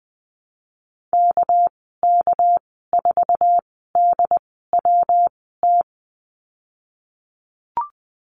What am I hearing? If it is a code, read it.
Morse code, heard as a keyed tone, about 20 wpm, spaced as for 14 wpm, 700 Hz. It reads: KK4DWT
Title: 231 Lesson 37g - Letter x - Call Signs 20wpm